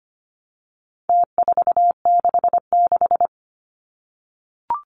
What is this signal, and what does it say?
Morse code, heard as a keyed tone, about 25 wpm, 700 Hz. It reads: T466
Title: Numbers - Contest Style - 4-Digit 25wpm